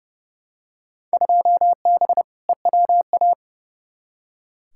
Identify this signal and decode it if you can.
Morse code, heard as a keyed tone, about 31 wpm, 700 Hz. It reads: 2B EWA
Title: Field Day Exchanges Encore - 30wpm